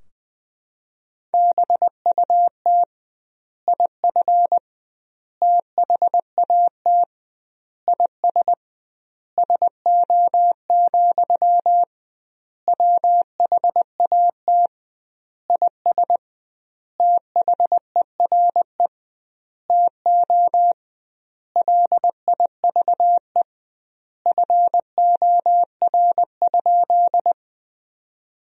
Morse code, heard as a keyed tone, about 20 wpm, 700 Hz. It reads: BUT IF THAT IS SO, WHAT IS THERE TO LIVE FOR?
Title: 2x Word-Spacing - Sentences from Top 200 Words - 20wpm